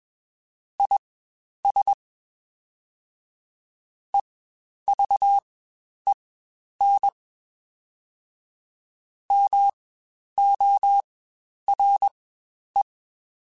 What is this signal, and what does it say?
Morse code, heard as a keyed tone, about 22 wpm, spaced as for 10 wpm, 800 Hz. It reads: IS EVEN MORE
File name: Spaced / Chuckles Spaced-mp3s / HumanErr sp @20